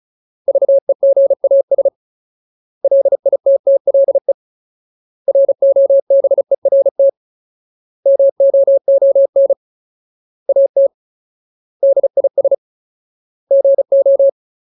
Morse code, heard as a keyed tone, about 34 wpm, 550 Hz. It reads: VEGAS LITTLE ROBERT MOON AT DIS GO
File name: Ultimate Word List / 35WPM Ultimate